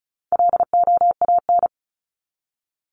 Morse code, heard as a keyed tone, about 34 wpm, 700 Hz. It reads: LOAN
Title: Top 1501-1750 Words 35wpm